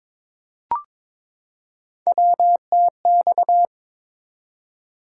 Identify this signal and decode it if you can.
Morse code, heard as a keyed tone, about 22 wpm, 700 Hz. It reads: WTX
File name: ARRL Sections v2 - 22wpm